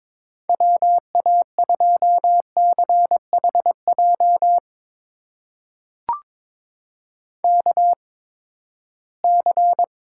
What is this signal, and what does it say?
Morse code, heard as a keyed tone, about 22 wpm, spaced as for 15 wpm, 700 Hz. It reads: WA2CHJ KC
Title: USA Call Signs - 6 Characters - 8x Character Spacing - 22wpm